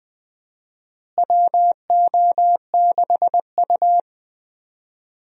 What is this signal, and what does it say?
Morse code, heard as a keyed tone, about 20 wpm, 700 Hz. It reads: WO6U
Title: USA Call Signs - 4 Characters - Repeated 3x - 20wpm